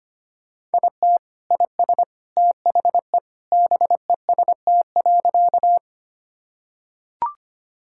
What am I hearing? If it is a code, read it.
Morse code, heard as a keyed tone, about 25 wpm, 700 Hz. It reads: IT IS THE BEST.
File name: practice / 25wpm / Sentences with 3-Word Phrases - 25wpm